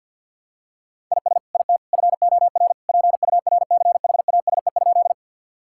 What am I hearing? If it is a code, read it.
Morse code, heard as a keyed tone, about 50 wpm, 700 Hz. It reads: IS IT FOR PURCHASE?
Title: Sentences from Top 1000 Words - Review Part 5 - 50wpm